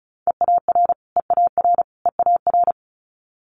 Morse code, heard as a keyed tone, about 36 wpm, 700 Hz. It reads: EAR EAR EAR
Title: Multi-letter Phonemes Repeated 3x 35wpm